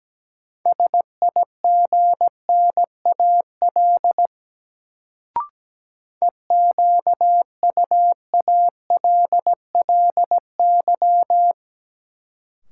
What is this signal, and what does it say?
Morse code, heard as a keyed tone, about 17 wpm, 700 Hz. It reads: SIGNAL EQUALLY
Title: Sentences from Top 1400 Words - New Word Drill - 17wpm